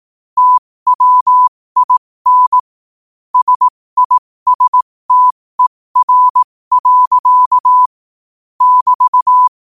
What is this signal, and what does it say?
Morse code, heard as a keyed tone, about 18 wpm, spaced as for 15 wpm, 1000 Hz. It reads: TWIN SISTER. <BT>
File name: qotdfiles / 15wpm